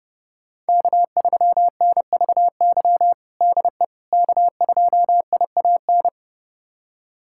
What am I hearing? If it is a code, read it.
Morse code, heard as a keyed tone, about 30 wpm, 700 Hz. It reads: K3NVY DE K2IAN
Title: USA Callsign DE Callsign - 5 Characters - 30wpm